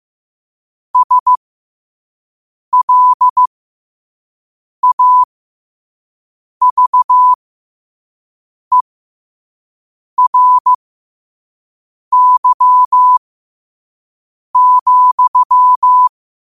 Morse code, heard as a keyed tone, about 15 wpm, spaced as for 5 wpm, 1000 Hz. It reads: SLAVERY,